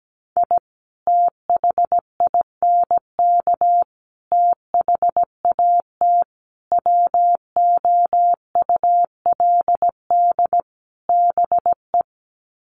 Morse code, heard as a keyed tone, about 17 wpm, 700 Hz. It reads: I THINK THAT WOULD BE
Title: Sets of 5 Words Encore2 17wpm